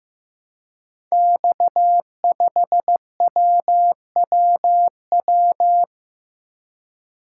Morse code, heard as a keyed tone, about 15 wpm, 700 Hz. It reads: X5WWW
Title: Code Groups - 5 Letter:Numbers - 15wpm